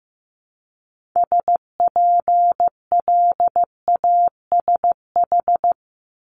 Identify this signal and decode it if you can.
Morse code, heard as a keyed tone, about 15 wpm, 700 Hz. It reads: SPLASH
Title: Multi-letter Phonemes 15wpm